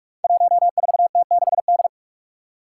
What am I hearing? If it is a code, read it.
Morse code, heard as a keyed tone, about 46 wpm, 700 Hz. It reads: 14T6D
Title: Five Challenging Letter-Number Combinations 45wpm